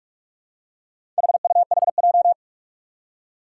Morse code, heard as a keyed tone, about 46 wpm, 700 Hz. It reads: HUHJ
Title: Four Challenging Letter-Number Combinations 45wpm